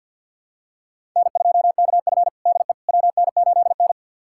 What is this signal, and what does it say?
Morse code, heard as a keyed tone, about 52 wpm, 700 Hz. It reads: N2KF DE WN8N
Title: USA Callsign DE Callsign - 4 Characters - 50wpm